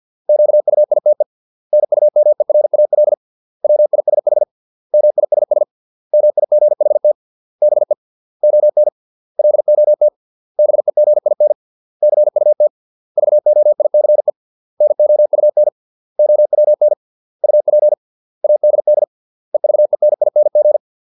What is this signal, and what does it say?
Morse code, heard as a keyed tone, about 50 wpm, 600 Hz. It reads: QUITE NUMERAL WISH MISS MIGHT BE ON LOT BEGIN CUT VOICE NOUN OWN UP ADD EVENING